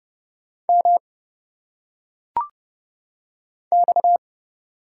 Morse code, heard as a keyed tone, about 31 wpm, 700 Hz. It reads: M X
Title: Single Letters - Phonetic Pronunciation 30wpm